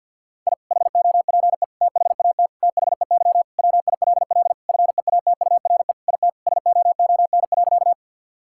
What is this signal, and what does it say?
Morse code, heard as a keyed tone, about 50 wpm, 700 Hz. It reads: I HOPE THAT THEY WILL FEATURE IT SOON.
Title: Sentences from Top 1000 Words - Review Part 5 - 50wpm